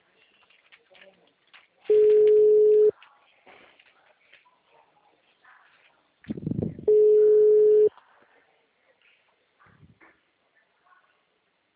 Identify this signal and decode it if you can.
Ringback tone (Europe, 425 Hz), with speech.